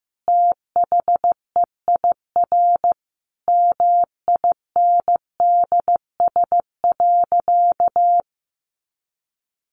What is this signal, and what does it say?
Morse code, heard as a keyed tone, about 15 wpm, 700 Hz. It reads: THEIR MINDS.
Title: Easy Phrases 15wpm